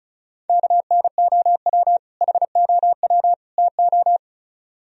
Morse code, heard as a keyed tone, about 45 wpm, 700 Hz. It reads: KNOW HOW TO